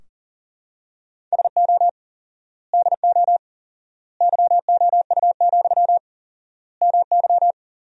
Morse code, heard as a keyed tone, about 37 wpm, 700 Hz. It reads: SO DO YOU, MY
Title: 4x Word-Spacing - Sentences from Top 400 Words - 40wpm